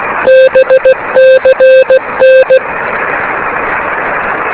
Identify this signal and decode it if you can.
Morse code, heard as a keyed tone, about 16 wpm, 520 Hz, with radio noise. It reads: BCN
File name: beacons / k5ab